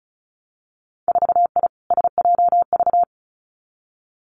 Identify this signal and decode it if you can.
Morse code, heard as a keyed tone, about 36 wpm, 700 Hz. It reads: 4I SJV
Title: Winter Field Day Exchanges Encore 35wpm